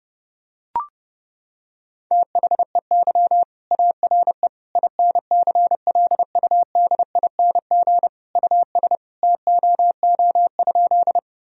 Morse code, heard as a keyed tone, about 32 wpm, 700 Hz. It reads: THEY ARE INCLUDING US TOO?
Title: Sentences from Top 1000 Words - Review Part 4 - 30wpm